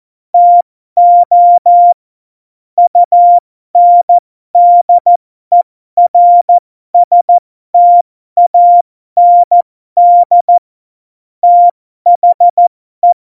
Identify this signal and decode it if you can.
Morse code, heard as a keyed tone, about 14 wpm, spaced as for 12 wpm, 700 Hz. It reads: TO UNDERSTAND THE
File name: Morsekurs / Witze